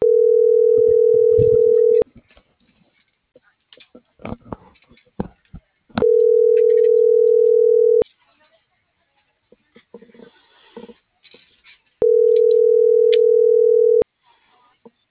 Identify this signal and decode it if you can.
Ringback tone (North America), 440 + 480 Hz.